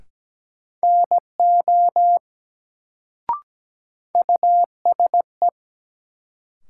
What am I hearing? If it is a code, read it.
Morse code, heard as a keyed tone, about 17 wpm, 700 Hz. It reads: NO USE
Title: Sentences from Top 100 Words - New Word Drill - 17wpm